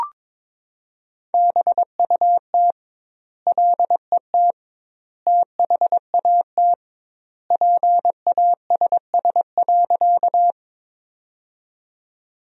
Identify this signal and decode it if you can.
Morse code, heard as a keyed tone, about 21 wpm, 700 Hz. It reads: BUT LET THAT PASS.
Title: 2x Word-Spacing - Sentences from Top 400 Words - 22wpm